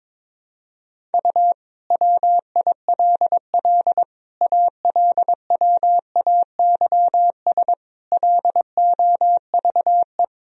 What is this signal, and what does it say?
Morse code, heard as a keyed tone, about 21 wpm, 700 Hz. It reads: U WILL ALWAYS LOVE
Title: Sentences from Top 400 Words and Abbreviations 22wpm